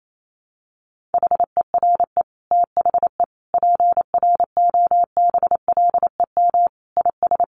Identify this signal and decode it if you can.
Morse code, heard as a keyed tone, about 28 wpm, 700 Hz. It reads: HERE THE PROBLEM IS
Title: Sentences from Top 1200 Words 28wpm